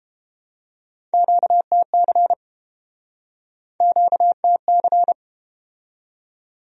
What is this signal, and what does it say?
Morse code, heard as a keyed tone, about 32 wpm, 700 Hz. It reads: QTC QTC